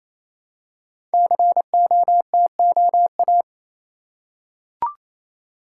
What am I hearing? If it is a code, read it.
Morse code, heard as a keyed tone, about 27 wpm, 700 Hz. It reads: COTOA
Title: Code Groups - 5 Letters - 28wpm